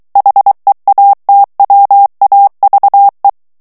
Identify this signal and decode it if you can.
Morse code, heard as a keyed tone, about 23 wpm, 800 Hz. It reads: HEATWAVE
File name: morse mp3 / ANGLAIS texte clair 2 vitesse 22 mots minute